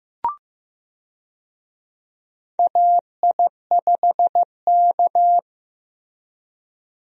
Morse code, heard as a keyed tone, about 15 wpm, 700 Hz. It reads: AI5K
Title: USA Call Signs - 4 Characters - Repeated 2x - 15wpm